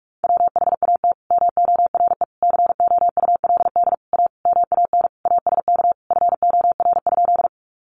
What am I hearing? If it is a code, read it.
Morse code, heard as a keyed tone, about 44 wpm, 700 Hz. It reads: WHAT MORE COULD A MAN ASK FOR?
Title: Sentences from Top 200 Words 45wpm